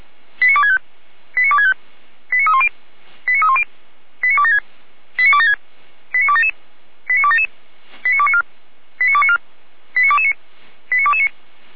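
Five-tone selective call (ZVEI), heard as quick runs of five tones, tones of about 70 ms. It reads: 89265 89265 89310 89310 89276 89276 89277 89277 89294 89294 89229 89229
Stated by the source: ZVEI five-tone call received by SDR.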